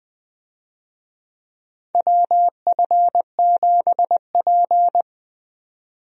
Morse code, heard as a keyed tone, about 20 wpm, 700 Hz. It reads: WF7P
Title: USA Call Signs - 4 Characters - 20wpm